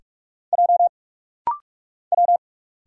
Morse code, heard as a keyed tone, about 46 wpm, 700 Hz. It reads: J W
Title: Single Letters - Rapid-Fire 45wpm